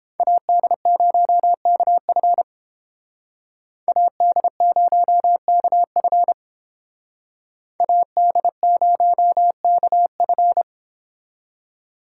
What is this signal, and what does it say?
Morse code, heard as a keyed tone, about 30 wpm, 700 Hz. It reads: AD0KF AD0KF AD0KF